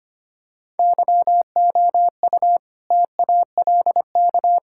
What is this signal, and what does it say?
Morse code, heard as a keyed tone, about 25 wpm, 700 Hz. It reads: YOU TALK
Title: Sentences from Top 1600 Words 25wpm